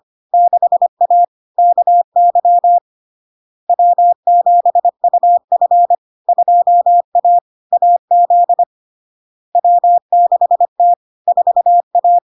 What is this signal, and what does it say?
Morse code, heard as a keyed tone, about 25 wpm, 700 Hz. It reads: 6A KY W7UF 2A AZ W6T 4A
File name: LARK-FD-2022-Log-in-CW-25wpm-700-Hz